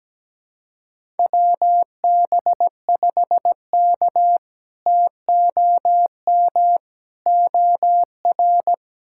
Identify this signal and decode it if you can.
Morse code, heard as a keyed tone, about 17 wpm, 700 Hz. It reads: WB5K TOM OR
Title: K1USN SST QSO Practice - 17wpm